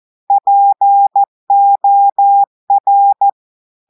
Morse code, heard as a keyed tone, about 14 wpm, 800 Hz. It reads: POR
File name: audios / portugais / portugais texte en clair 1 vitesse 14 mots minutes